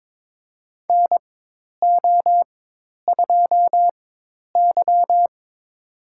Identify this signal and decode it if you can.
Morse code, heard as a keyed tone, about 22 wpm, spaced as for 10 wpm, 700 Hz. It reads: NO2Y
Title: USA Call Signs - 4x Character Spacing - 22wpm